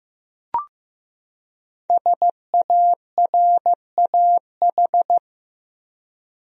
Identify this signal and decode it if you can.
Morse code, heard as a keyed tone, about 15 wpm, 700 Hz. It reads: SARAH
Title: USA Common Names - Max 5 Letters - 15wpm